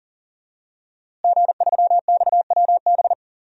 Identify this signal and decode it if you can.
Morse code, heard as a keyed tone, about 40 wpm, 700 Hz. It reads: G3XWB
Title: DXCC-100 Call Signs - 5 Characters - Repeated 3x - 40wpm